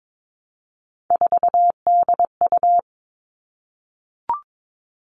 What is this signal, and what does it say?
Morse code, heard as a keyed tone, about 22 wpm, 700 Hz. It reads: 4DU
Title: All Cut Numbers - 3-Digit 22wpm